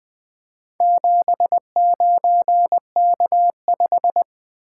Morse code, heard as a keyed tone, about 20 wpm, 700 Hz. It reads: 79K5
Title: Four Letter-Number Combinations - 20wpm